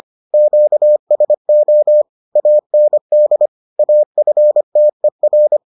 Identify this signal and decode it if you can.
Morse code, heard as a keyed tone, about 25 wpm, 600 Hz. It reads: QSO AND AFTER